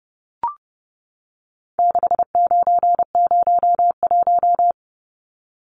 Morse code, heard as a keyed tone, about 30 wpm, 700 Hz. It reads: 6901
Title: Numbers - 4-Digit 30wpm